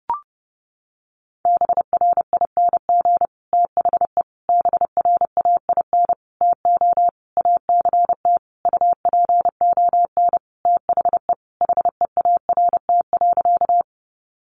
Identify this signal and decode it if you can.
Morse code, heard as a keyed tone, about 30 wpm, 700 Hz. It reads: BRING THE BRAIN TO ACT UPON THE HEART.
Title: Sentences from Top 1200 Words 30wpm